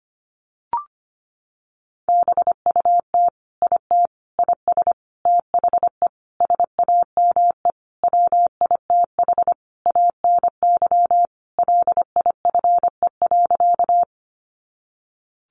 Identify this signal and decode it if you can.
Morse code, heard as a keyed tone, about 25 wpm, 700 Hz. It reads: BUT IT IS THE SAME WITH ANY LIFE.
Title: Sentences from Top 300 Words 25wpm